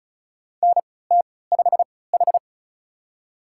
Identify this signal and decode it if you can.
Morse code, heard as a keyed tone, about 36 wpm, spaced as for 20 wpm, 700 Hz. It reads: NT5H
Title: USA Call Signs - 4 Characters - 3x Character Spacing - 35wpm